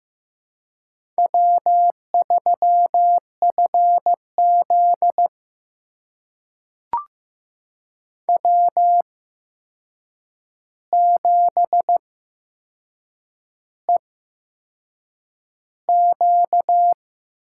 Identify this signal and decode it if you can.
Morse code, heard as a keyed tone, about 15 wpm, spaced as for 6 wpm, 700 Hz. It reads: W3FZ W7EQ
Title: USA Call Signs - 4 Characters - 8x Character Spacing - 15wpm